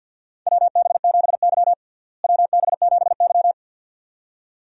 Morse code, heard as a keyed tone, about 49 wpm, 700 Hz. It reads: WB7Y WB7Y
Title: USA Call Signs - 4 Characters - Repeated 2x - 50wpm